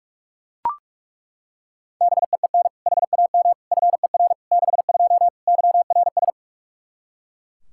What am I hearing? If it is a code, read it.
Morse code, heard as a keyed tone, about 42 wpm, 700 Hz. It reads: BEEN HAM FER 62 YRS
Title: K1USN SST QSO Practice 45wpm